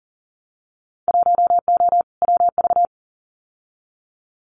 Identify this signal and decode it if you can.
Morse code, heard as a keyed tone, about 40 wpm, 700 Hz. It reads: 1O WV
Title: Winter Field Day Exchanges Encore 40wpm